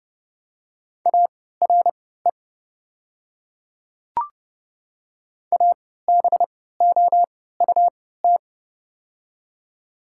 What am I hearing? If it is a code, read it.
Morse code, heard as a keyed tone, about 30 wpm, spaced as for 17 wpm, 700 Hz. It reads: ARE ABOUT
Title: 3x Character Spacing - Top 200 Words - 30wpm